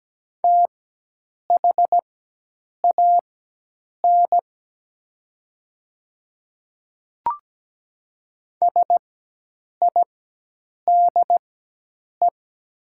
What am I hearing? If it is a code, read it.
Morse code, heard as a keyed tone, about 17 wpm, spaced as for 8 wpm, 700 Hz. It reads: THAN SIDE
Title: 4x Character Spacing - Top 100 Words - 17wpm